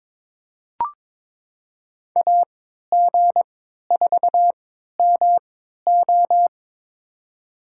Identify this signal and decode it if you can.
Morse code, heard as a keyed tone, about 23 wpm, spaced as for 13 wpm, 700 Hz. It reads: AG4MO
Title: USA Call Signs - 5 Characters - 3x Character Spacing - 22wpm